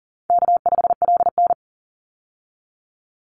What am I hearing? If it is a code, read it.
Morse code, heard as a keyed tone, about 40 wpm, 700 Hz. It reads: K5LN
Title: Easy Call Signs Repeated 3x - 40wpm